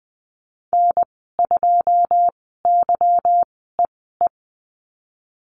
Morse code, heard as a keyed tone, about 20 wpm, spaced as for 15 wpm, 700 Hz. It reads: N2YEE
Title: 193 Lesson 31g - Letter v - Call Signs 20wpm